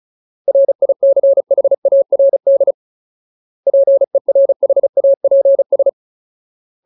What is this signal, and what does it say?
Morse code, heard as a keyed tone, about 36 wpm, 550 Hz. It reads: RICHARD PERHAPS
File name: Ultimate Word List / 35WPM Ultimate